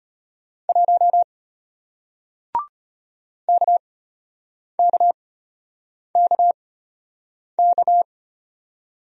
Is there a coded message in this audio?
Yes: Morse code from 0.7 to 8.0 s, heard as a keyed tone, about 34 wpm, spaced as for 8 wpm, 700 Hz. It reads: 1 KKKK